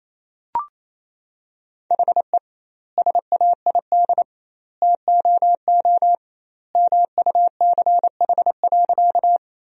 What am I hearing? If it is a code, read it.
Morse code, heard as a keyed tone, about 28 wpm, 700 Hz. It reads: HE SAID TOO MUCH.
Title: 2x Word-Spacing - Sentences from Top 200 Words - 28wpm